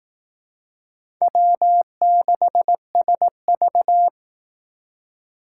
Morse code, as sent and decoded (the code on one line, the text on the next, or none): .-- -.... ... ...-
W6SV